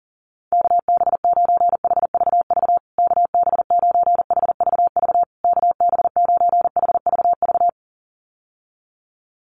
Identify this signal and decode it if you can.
Morse code, heard as a keyed tone, about 40 wpm, 700 Hz. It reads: KB9HVV KB9HVV KB9HVV
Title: USA Call Signs Repeated 3x 40wpm